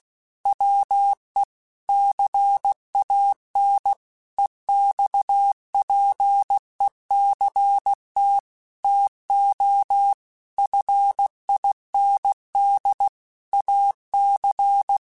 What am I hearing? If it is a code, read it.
Morse code, heard as a keyed tone, about 16 wpm, 800 Hz. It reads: WE CAN EXPECT TO FIND AC
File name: Technical Subjects / Technical Subjects-mp3s / Sw Pwr Sup @15